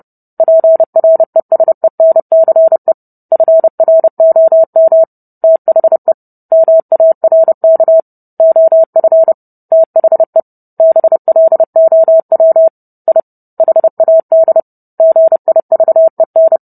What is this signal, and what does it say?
Morse code, heard as a keyed tone, about 30 wpm, 650 Hz. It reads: PRESENCE FROM THE MARK OF THE BLOW I HAD GIVEN